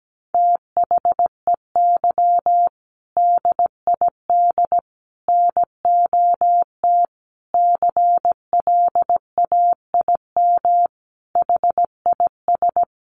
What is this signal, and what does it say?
Morse code, heard as a keyed tone, about 17 wpm, 700 Hz. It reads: THEY DID NOT CLAIM HIS
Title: Sentences from Top 1500 Words 17wpm